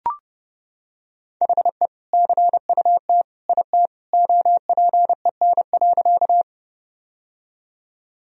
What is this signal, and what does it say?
Morse code, heard as a keyed tone, about 30 wpm, 700 Hz. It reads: HE CUT IT OPEN.
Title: Sentences from Top 1000 Words - Review Part 1 - 30wpm